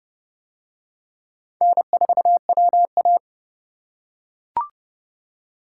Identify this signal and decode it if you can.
Morse code, heard as a keyed tone, about 29 wpm, 700 Hz. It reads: N4WA